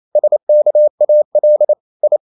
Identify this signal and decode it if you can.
Morse code, heard as a keyed tone, about 28 wpm, 600 Hz. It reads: SKAL I
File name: Ferie QRQ 27wpm